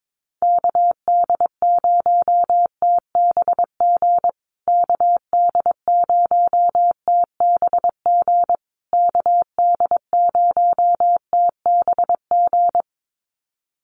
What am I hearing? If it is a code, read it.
Morse code, heard as a keyed tone, about 22 wpm, 700 Hz. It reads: KD0TBG KD0TBG KD0TBG
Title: USA Call Signs Repeated 3x 22wpm